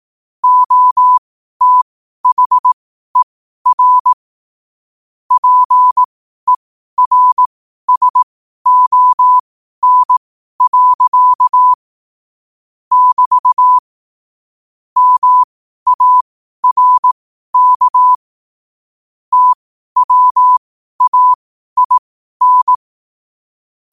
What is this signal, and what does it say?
Morse code, heard as a keyed tone, about 18 wpm, spaced as for 12 wpm, 1000 Hz. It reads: OTHER PERSON. <BT> MARK TWAIN